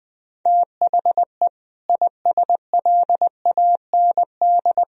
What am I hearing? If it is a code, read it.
Morse code, heard as a keyed tone, about 20 wpm, 700 Hz. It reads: THE ISLAND
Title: Sentences from Top 500 Words and Abbreviations 20wpm